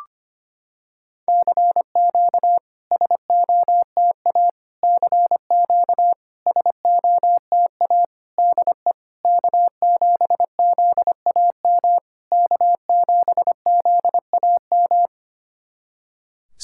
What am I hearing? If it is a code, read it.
Morse code, heard as a keyed tone, about 25 wpm, 700 Hz. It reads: CQ SOTA CQ SOTA DE K7ZAM K7ZAM
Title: SOTA QSO Practice 25wpm